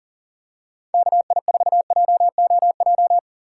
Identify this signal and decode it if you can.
Morse code, heard as a keyed tone, about 40 wpm, 700 Hz. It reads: KI4JOJ
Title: USA Call Signs - 6 Characters - 4x Character Spacing - 40wpm